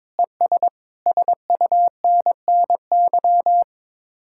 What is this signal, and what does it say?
Morse code, heard as a keyed tone, about 22 wpm, 700 Hz. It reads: ES SUNNY
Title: Speed-Racing Common QSO Phrases - 22wpm